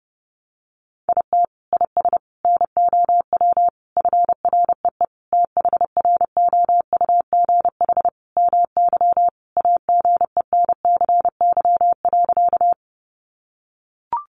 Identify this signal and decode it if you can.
Morse code, heard as a keyed tone, about 30 wpm, 700 Hz. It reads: IT IS NOW FREE THROUGH MY AGENCY.
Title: Sentences from Top 1300 Words 30wpm